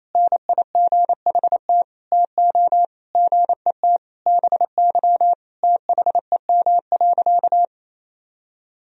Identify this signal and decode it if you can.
Morse code, heard as a keyed tone, about 28 wpm, 700 Hz. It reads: NIGHT TO GET BY THEM.
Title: Sentences from Top 1000 Words - Review Part 1 - 28wpm